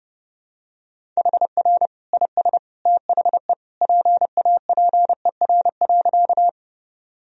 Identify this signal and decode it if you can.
Morse code, heard as a keyed tone, about 30 wpm, 700 Hz. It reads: HR IS THE PAPER.
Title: Sentences from Top 300 Words and Abbreviations 30wpm